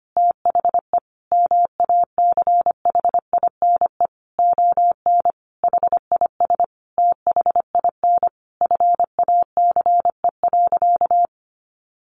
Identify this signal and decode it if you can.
Morse code, heard as a keyed tone, about 25 wpm, 700 Hz. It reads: THE MACHINE ON HIS THIN FACE.
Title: A Sound Of Thunder 25wpm